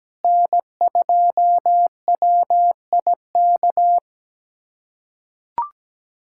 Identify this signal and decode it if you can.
Morse code, heard as a keyed tone, about 17 wpm, 700 Hz. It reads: N2WIK
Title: USA Callsign DE Callsign - 5 Characters - 17wpm